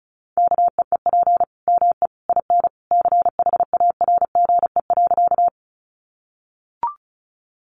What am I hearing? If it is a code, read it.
Morse code, heard as a keyed tone, about 34 wpm, 700 Hz. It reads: KEEP ME IN CHARGE.
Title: Sentences from Top 900 Words 35wpm